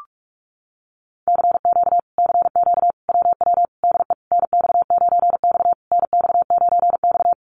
Morse code, heard as a keyed tone, about 44 wpm, 700 Hz. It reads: CQ CQ WW DE NX9X NX9X
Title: Common QSO Phrases Encore 45wpm